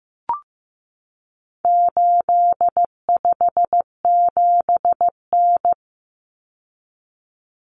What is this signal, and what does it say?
Morse code, heard as a keyed tone, about 15 wpm, 700 Hz. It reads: 857N
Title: Common Cut Numbers - 4-Digit 15wpm